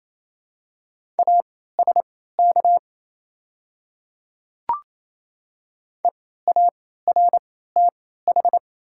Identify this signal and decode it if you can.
Morse code, heard as a keyed tone, about 28 wpm, spaced as for 16 wpm, 700 Hz. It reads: ASK EARTH
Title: 3x Character Spacing - Top 200 Words - 28wpm